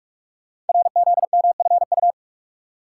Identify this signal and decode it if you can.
Morse code, heard as a keyed tone, about 44 wpm, 700 Hz. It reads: RZMFU
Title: Code Groups - 5 Letter:Numbers - 45wpm